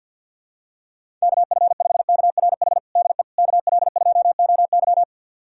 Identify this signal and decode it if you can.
Morse code, heard as a keyed tone, about 52 wpm, 700 Hz. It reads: KF5KRH DE KL2OY
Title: USA Callsign DE Callsign 50wpm